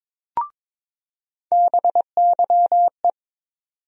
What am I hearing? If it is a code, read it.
Morse code, heard as a keyed tone, about 22 wpm, 700 Hz. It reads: BYE
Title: USA Call Sign Postfixes - 3 Letters 22wpm